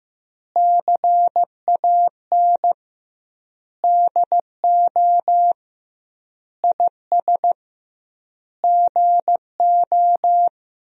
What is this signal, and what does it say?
Morse code, heard as a keyed tone, about 15 wpm, 700 Hz. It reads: CAN DO IS GO
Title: 2x Word-Spacing - Sentences from Top 200 Words - 15wpm